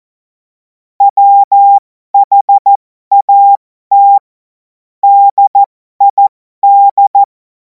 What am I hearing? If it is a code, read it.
Morse code, heard as a keyed tone, about 14 wpm, spaced as for 12 wpm, 800 Hz. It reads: WHAT DID
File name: Morsekurs / Witze